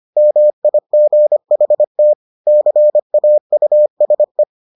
Morse code, heard as a keyed tone, about 25 wpm, 600 Hz. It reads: MIGHT CAUSE